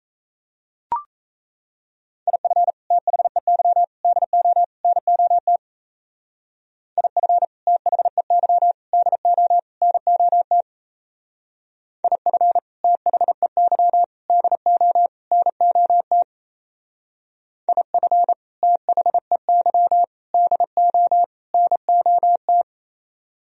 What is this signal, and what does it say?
Morse code, heard as a keyed tone, about 33 wpm, 700 Hz. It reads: IF THEY DO NOT IF THEY DO NOT IF THEY DO NOT IF THEY DO NOT
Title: Speed-Racing - Sets of 4 Words 28wpm